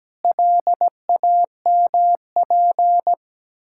Morse code, heard as a keyed tone, about 17 wpm, 700 Hz. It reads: LAMP
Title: Pangram Challenge v2 - 17wpm